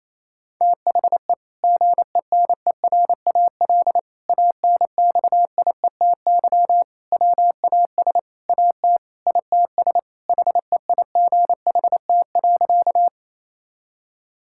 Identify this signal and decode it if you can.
Morse code, heard as a keyed tone, about 28 wpm, 700 Hz. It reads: THE GENERAL ANXIETY WAS AT ITS HEIGHT.
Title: Sentences from Top 1600 Words 28wpm